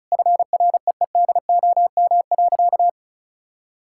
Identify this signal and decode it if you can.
Morse code, heard as a keyed tone, about 36 wpm, 700 Hz. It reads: FREEDOM.
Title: Sentences from Top 1000 Words - Review Part 5 - 35wpm